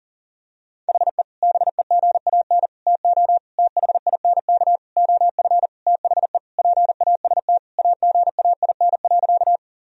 Morse code, heard as a keyed tone, about 40 wpm, 700 Hz. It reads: HE BEGAN TO THINK OF THE PAST AGAIN.
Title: Sentences from Top 1000 Words - Review Part 3 - 40wpm